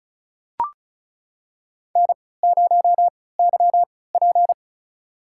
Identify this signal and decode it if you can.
Morse code, heard as a keyed tone, about 34 wpm, spaced as for 20 wpm, 700 Hz. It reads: N0YP
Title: USA Call Signs - 4 Characters - 3x Character Spacing - 35wpm